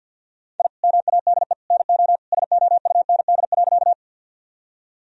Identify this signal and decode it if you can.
Morse code, heard as a keyed tone, about 50 wpm, 700 Hz. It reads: I MADE NO SOUND.
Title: Sentences from Top 1000 Words - Review Part 1 - 50wpm